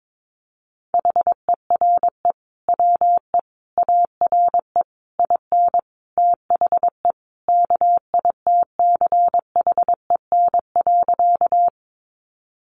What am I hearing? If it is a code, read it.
Morse code, heard as a keyed tone, about 22 wpm, 700 Hz. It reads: HERE WE ARE IN THE KITCHEN.
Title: Sentences from Top 1400 Words 22wpm